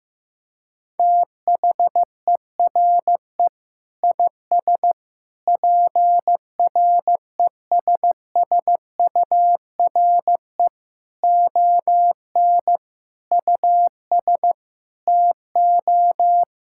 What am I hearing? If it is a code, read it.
Morse code, heard as a keyed tone, about 16 wpm, 700 Hz. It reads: THERE IS PRESSURE ON US TO
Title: Sentences from Top 1000 Words - Review Part 6 - 15wpm